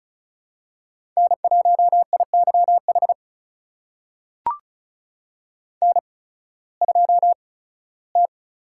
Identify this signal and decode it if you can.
Morse code, heard as a keyed tone, about 35 wpm, spaced as for 19 wpm, 700 Hz. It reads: N1IYH N2T
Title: USA Call Signs - 5 Characters - 8x Character Spacing - 35wpm